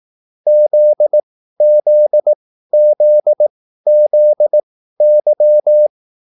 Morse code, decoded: ZZZZY

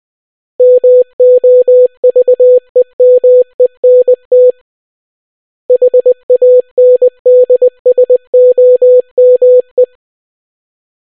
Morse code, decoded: MOVEMENT HANDSOME

